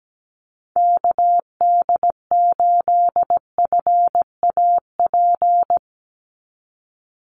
Morse code, heard as a keyed tone, about 17 wpm, 700 Hz. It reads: KD8FAP
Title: USA Call Signs 17wpm